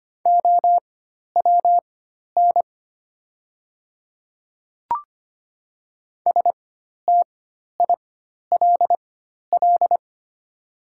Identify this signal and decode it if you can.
Morse code, heard as a keyed tone, about 25 wpm, spaced as for 12 wpm, 700 Hz. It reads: OWN STILL
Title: 4x Character Spacing - Top 300 Words - 25wpm